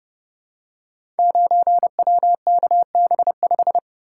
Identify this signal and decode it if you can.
Morse code, heard as a keyed tone, about 29 wpm, 700 Hz. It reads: 9WKB5
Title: Code Groups - 5 Letter:Numbers - 30wpm